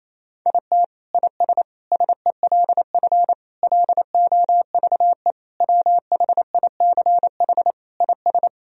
Morse code, heard as a keyed tone, about 28 wpm, 700 Hz. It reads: IT IS SELF LOVE WHICH IS
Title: Sentences from Top 1000 Words - Review Part 1 - 28wpm